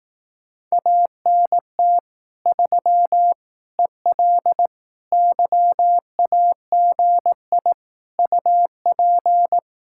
Morse code, heard as a keyed tone, about 18 wpm, 700 Hz. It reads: ANT 3 EL YAGI UP
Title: Speed-Racing Common QSO Phrases Encore - 15wpm